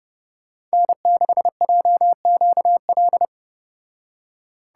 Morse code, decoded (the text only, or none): N6JQL